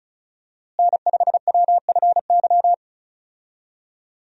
Morse code, heard as a keyed tone, about 36 wpm, 700 Hz. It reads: N5WFY
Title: USA Call Signs - 5 Characters - Repeated 3x - 35wpm